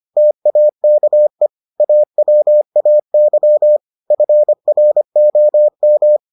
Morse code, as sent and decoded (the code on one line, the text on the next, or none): - .- -.- . / .- .-- .- -.-- / ..-. .-. --- --
TAKE AWAY FROM